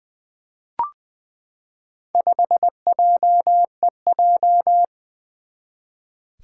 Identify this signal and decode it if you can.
Morse code, heard as a keyed tone, about 20 wpm, 700 Hz. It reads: 5JEJ
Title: Four Challenging Letter-Number Combinations 20wpm